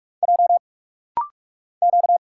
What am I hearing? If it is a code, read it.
Morse code, heard as a keyed tone, about 45 wpm, 700 Hz. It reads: J Q
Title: Single Letters - Rapid-Fire 45wpm